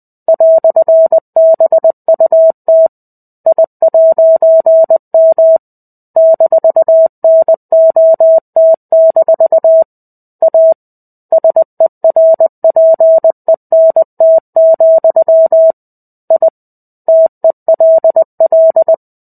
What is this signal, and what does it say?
Morse code, heard as a keyed tone, about 20 wpm, 650 Hz. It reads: "BUT I'M -NOT- A SERPENT, I TELL
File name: 05-Chapter-05 20 WPM